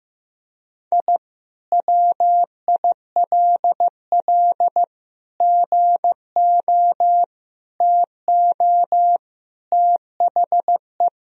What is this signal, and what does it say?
Morse code, heard as a keyed tone, about 15 wpm, 700 Hz. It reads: I WILL GO TO THE
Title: Sentences from Top 1000 Words - Review Part 1 - 15wpm